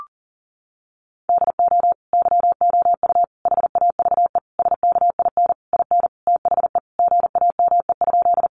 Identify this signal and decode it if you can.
Morse code, heard as a keyed tone, about 40 wpm, 700 Hz. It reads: DO YOU HAVE SKIN IN THE GAME?
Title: Sentences from Top 1300 Words 40wpm